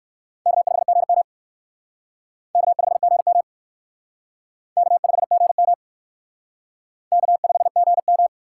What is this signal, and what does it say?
Morse code, heard as a keyed tone, about 56 wpm, 700 Hz. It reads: K5GM K5GM K5GM K5GM
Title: Speed Racing - USA 1x2 Call Signs 45wpm